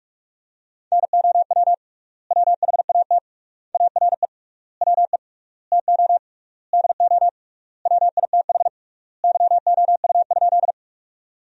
Morse code, heard as a keyed tone, about 46 wpm, 700 Hz. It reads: NOW WHAT ARE WE TO DO WITH YOU?
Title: 3x Word-Spacing - Sentences from Top 100 Words - 45wpm